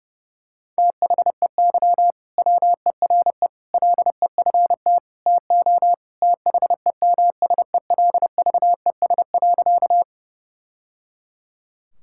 Morse code, heard as a keyed tone, about 30 wpm, 700 Hz. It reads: THEY WERE LEFT TO THEMSELVES.
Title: Sentences from Top 1000 Words - Review Part 5 - 30wpm